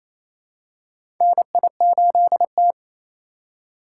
Morse code, heard as a keyed tone, about 28 wpm, 700 Hz. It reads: NI8T
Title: USA Call Signs - 4 Characters - 3x Character Spacing - 28wpm